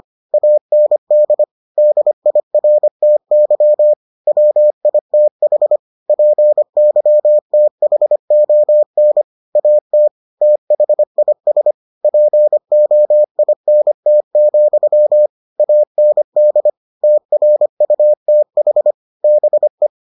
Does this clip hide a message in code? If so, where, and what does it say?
Yes: Morse code from 0.3 to 19.9 s, heard as a keyed tone, about 25 wpm, 600 Hz. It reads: AND DIRTY WITH PYTHON AT THIS POINT, AND TRUTH BE